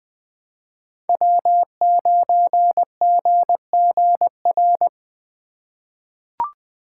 Morse code, heard as a keyed tone, about 20 wpm, 700 Hz. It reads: W9GGR
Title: USA Call Signs - 5 Characters - 4x Character Spacing - 20wpm